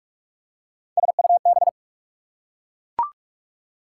Morse code, heard as a keyed tone, about 44 wpm, 690 Hz. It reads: SUB
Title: Prefixes and Suffixes 45wpm